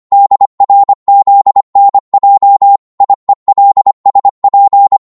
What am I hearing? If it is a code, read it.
Morse code, heard as a keyed tone, about 25 wpm, 800 Hz. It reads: DRZNJ IELSP
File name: Morse / ltr-25wpm